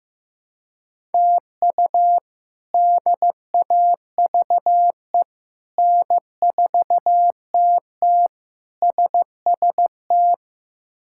Morse code, heard as a keyed tone, about 16 wpm, 700 Hz. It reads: TU DAVE N4TT SST
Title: K1USN SST QSO Practice - 15wpm